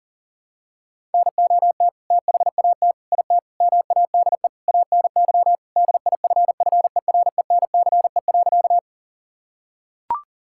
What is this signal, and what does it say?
Morse code, heard as a keyed tone, about 40 wpm, 700 Hz. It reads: NOT THAT IT MADE ANY DIFFERENCE.
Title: Sentences from Top 1000 Words - Review Part 5 - 40wpm